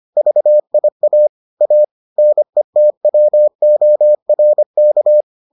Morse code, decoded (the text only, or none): VIA A NETWORK